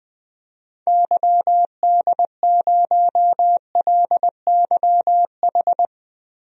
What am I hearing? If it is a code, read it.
Morse code, heard as a keyed tone, about 20 wpm, 700 Hz. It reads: YD0LYH
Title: DXCC-100 Call Signs - 6 Characters - Repeated 2x - 20wpm